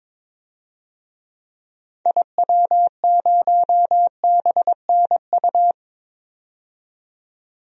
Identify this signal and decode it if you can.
Morse code, heard as a keyed tone, about 22 wpm, 700 Hz. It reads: IW0BNU
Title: DXCC-100 Call Signs - 6 Characters - 22wpm